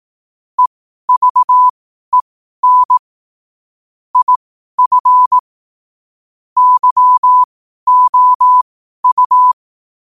Morse code, heard as a keyed tone, about 18 wpm, spaced as for 12 wpm, 1000 Hz. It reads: EVEN IF YOU
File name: qotdfiles / 12wpm